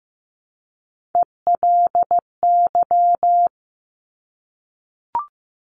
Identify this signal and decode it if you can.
Morse code, heard as a keyed tone, about 15 wpm, 700 Hz. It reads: ELY
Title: Top Three Letter Combinations in Words 15wpm